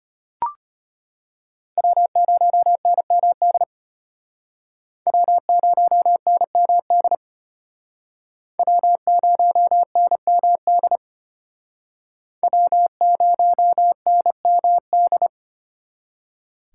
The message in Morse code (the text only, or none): W0NMD W0NMD W0NMD W0NMD